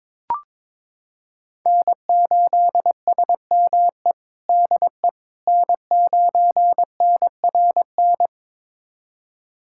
Morse code, heard as a keyed tone, about 22 wpm, 700 Hz. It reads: N8SME DE N9NRN
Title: USA Callsign DE Callsign - 5 Characters - 22wpm